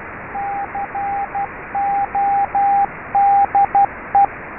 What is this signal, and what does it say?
Morse code, heard as a keyed tone, about 12 wpm, 800 Hz, with radio noise. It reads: CODE
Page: MORSE CODE TRAINER AND SIMULATOR